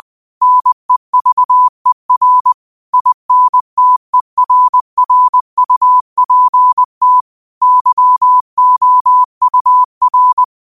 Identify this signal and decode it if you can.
Morse code, heard as a keyed tone, about 20 wpm, 1000 Hz. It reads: NEVER INTERRUPT YOUR
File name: qotdfiles / 20wpm